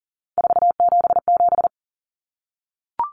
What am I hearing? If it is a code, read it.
Morse code, heard as a keyed tone, about 40 wpm, 700 Hz. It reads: 477
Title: Common Cut Numbers - 3-Digit 40wpm